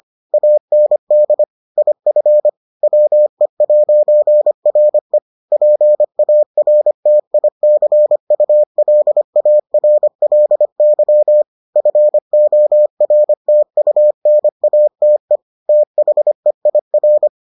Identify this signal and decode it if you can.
Morse code, heard as a keyed tone, about 25 wpm, 600 Hz. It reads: AND IF WE'RE PARTICULARLY FORTUNATE THEIR